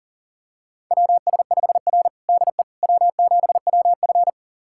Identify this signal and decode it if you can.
Morse code, heard as a keyed tone, about 39 wpm, 700 Hz. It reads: WS5R DE W7WF
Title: USA Callsign DE Callsign - 4 Characters - 40wpm